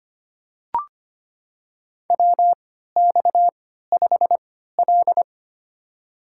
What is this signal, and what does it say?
Morse code, heard as a keyed tone, about 25 wpm, spaced as for 14 wpm, 700 Hz. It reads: WX5L
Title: USA Call Signs - 4 Characters - 3x Character Spacing - 25wpm